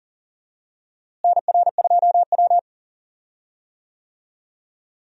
Morse code, heard as a keyed tone, about 40 wpm, 700 Hz. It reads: NR2W
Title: USA Call Signs - 4 Characters - Repeated 3x - 40wpm